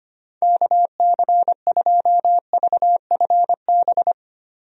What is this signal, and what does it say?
Morse code, heard as a keyed tone, about 25 wpm, 700 Hz. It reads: KC2VFB
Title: USA Call Signs - 6 Characters - Repeated 2x - 25wpm